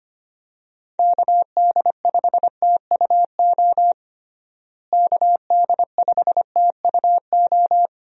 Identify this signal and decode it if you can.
Morse code, heard as a keyed tone, about 25 wpm, 700 Hz. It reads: KD5TUO KD5TUO
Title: USA Call Signs - 6 Characters - Repeated 2x - 25wpm